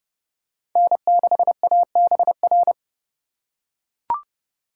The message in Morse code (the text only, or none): N6ABR